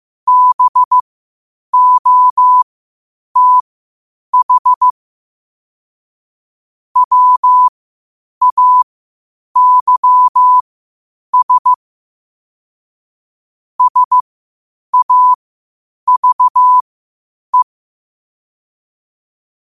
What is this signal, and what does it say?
Morse code, heard as a keyed tone, about 15 wpm, spaced as for 8 wpm, 1000 Hz. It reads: BOTH WAYS SAVE
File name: qotdfiles / 8wpm